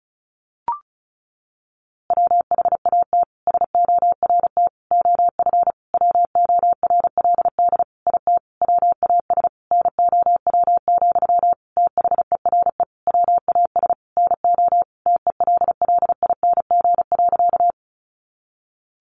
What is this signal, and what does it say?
Morse code, heard as a keyed tone, about 34 wpm, 700 Hz. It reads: WHAT SORT OF WORLD IT WAS NOW, THERE WAS NO TELLING.
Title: A Sound Of Thunder 35wpm